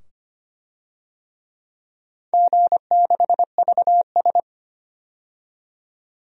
Morse code, heard as a keyed tone, about 25 wpm, 700 Hz. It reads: G6VS
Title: DXCC-100 Call Signs - 4 Characters - 25wpm